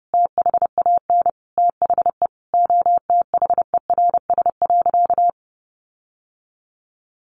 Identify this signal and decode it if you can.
Morse code, heard as a keyed tone, about 30 wpm, 700 Hz. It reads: THAN THE OTHERS.
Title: Sentences from Top 1500 Words 30wpm